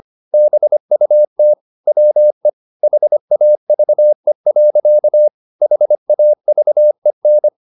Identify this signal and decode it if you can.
Morse code, heard as a keyed tone, about 25 wpm, 600 Hz. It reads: BUT WE HAVE. HAVEN
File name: the old man and the sea part 1 25wpm